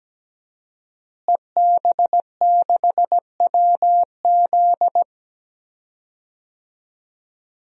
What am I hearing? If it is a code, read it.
Morse code, heard as a keyed tone, about 17 wpm, 700 Hz. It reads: EB6WZ
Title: DXCC-100 Call Signs - 5 Characters - Repeated 3x - 17wpm